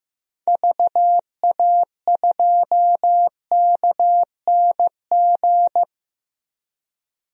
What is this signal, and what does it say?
Morse code, heard as a keyed tone, about 15 wpm, 700 Hz. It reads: VA2KNG
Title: DXCC-100 Call Signs - 6 Characters - Repeated 2x - 15wpm